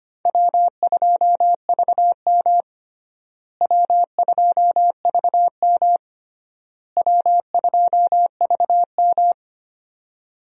Morse code, heard as a keyed tone, about 25 wpm, 700 Hz. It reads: W2VM W2VM W2VM
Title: USA Call Signs - 4 Characters - Repeated 3x - 25wpm